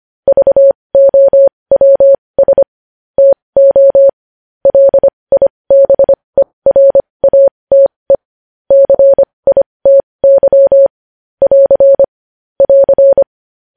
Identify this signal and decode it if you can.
Morse code, heard as a keyed tone, about 25 wpm, spaced as for 20 wpm, 550 Hz. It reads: VOWS TO LIBERATE CITY <AR> <AR>
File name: podcast / News headlines-1x-25x20